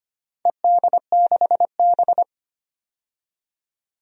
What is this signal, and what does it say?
Morse code, heard as a keyed tone, about 25 wpm, 700 Hz. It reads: ED6B
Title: DXCC-100 Call Signs - 4 Characters - 25wpm